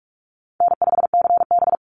Morse code, heard as a keyed tone, about 44 wpm, 700 Hz. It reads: N5CB
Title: USA Call Signs 45wpm